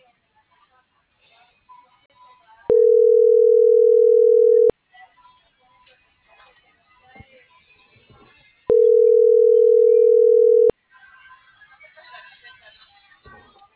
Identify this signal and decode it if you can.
Ringback tone (North America), 440 + 480 Hz.